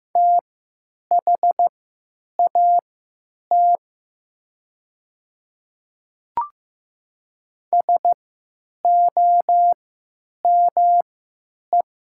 Morse code, heard as a keyed tone, about 15 wpm, spaced as for 8 wpm, 700 Hz. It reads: THAT SOME
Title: 3x Character Spacing - Top 200 Words - 15wpm